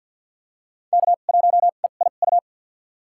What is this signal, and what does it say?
Morse code, heard as a keyed tone, about 51 wpm, spaced as for 36 wpm, 700 Hz. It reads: K1EIU
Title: USA Call Signs - 5 Characters - 2x Character Spacing - 50wpm